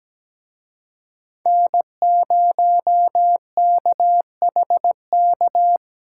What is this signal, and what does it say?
Morse code, heard as a keyed tone, about 20 wpm, 700 Hz. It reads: N0KHK
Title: Speed Racing - USA Call Signs 17wpm